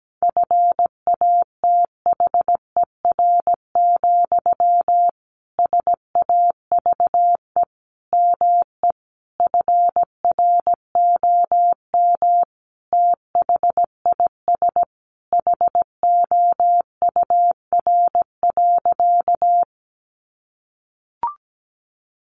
Morse code, heard as a keyed tone, about 17 wpm, 700 Hz. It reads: FATHER, SAVE ME FROM THIS HOUR.
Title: Sentences from Top 1100 Words 17wpm